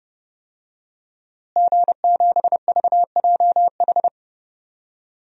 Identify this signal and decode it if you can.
Morse code, heard as a keyed tone, about 30 wpm, 700 Hz. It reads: G7VJH
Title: DXCC-100 Call Signs - 5 Characters - Repeated 3x - 30wpm